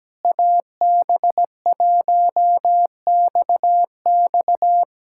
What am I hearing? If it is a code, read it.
Morse code, heard as a keyed tone, about 17 wpm, 700 Hz. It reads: AB1XX
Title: K1USN SST QSO Practice - 17wpm